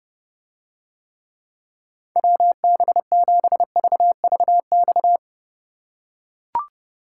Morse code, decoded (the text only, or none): WB7VVX